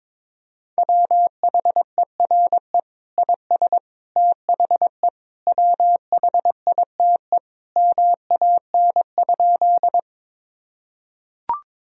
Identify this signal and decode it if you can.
Morse code, heard as a keyed tone, about 22 wpm, 700 Hz. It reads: WHERE IS THE WHITE MAN?
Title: Sentences from Top 1000 Words - Review Part 1 - 22wpm